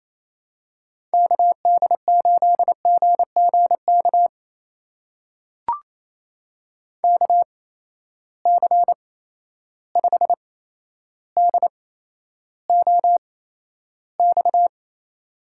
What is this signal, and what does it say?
Morse code, heard as a keyed tone, about 28 wpm, spaced as for 12 wpm, 700 Hz. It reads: KD8GGK KC5DOX
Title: USA Call Signs - 6 Characters - 8x Character Spacing - 28wpm